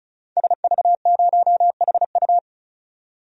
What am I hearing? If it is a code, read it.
Morse code, heard as a keyed tone, about 36 wpm, 700 Hz. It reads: SV0HU
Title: DXCC-100 Call Signs - 5 Characters - 35wpm